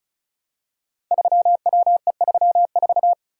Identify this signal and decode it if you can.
Morse code, heard as a keyed tone, about 36 wpm, 700 Hz. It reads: 3WE34
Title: Code Groups - 5 Letter:Numbers - 35wpm